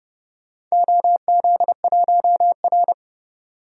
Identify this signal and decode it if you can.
Morse code, heard as a keyed tone, about 30 wpm, 700 Hz. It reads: OZ1R
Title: DXCC-100 Call Signs - 4 Characters - 30wpm